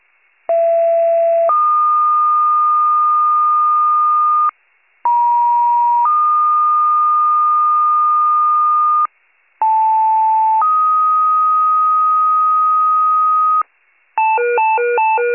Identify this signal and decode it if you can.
Two-tone page, heard as a one-second tone, then a three-second tone, about 670 then 1220 Hz, about 950 then 1220 Hz, about 880 then 1250 Hz. It is heard from 0.5 to 13.6 s.